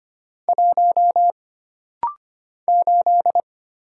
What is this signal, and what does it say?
Morse code, heard as a keyed tone, about 25 wpm, 700 Hz. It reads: 1 8
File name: Single Character (Letter:Number:Punctuation:Prosign) - Rapid-Fire - 25wpm